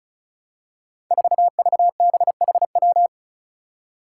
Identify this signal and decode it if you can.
Morse code, heard as a keyed tone, about 36 wpm, 700 Hz. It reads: 4VBHW
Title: Five Challenging Letter-Number Combinations 35wpm